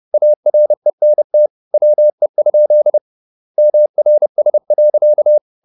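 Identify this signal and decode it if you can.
Morse code, heard as a keyed tone, about 30 wpm, 600 Hz. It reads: ARENT WE? MRS.
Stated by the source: Morse code at 30 wpm, the novel Never by Ken Follett in English, from CW med Gnister.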